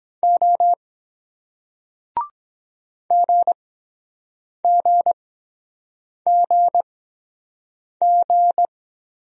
Morse code, heard as a keyed tone, about 23 wpm, spaced as for 7 wpm, 700 Hz. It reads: O GGGG